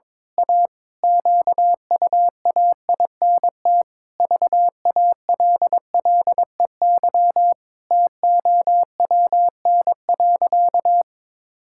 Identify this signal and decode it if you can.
Morse code, heard as a keyed tone, about 22 wpm, 700 Hz. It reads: A QUAINT VALLEY TOWN.
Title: Pangram Challenge v2 - 22wpm